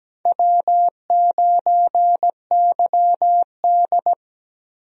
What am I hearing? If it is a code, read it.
Morse code, heard as a keyed tone, about 17 wpm, 700 Hz. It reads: W9YD
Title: USA Call Signs - 4 Characters - Repeated 3x - 17wpm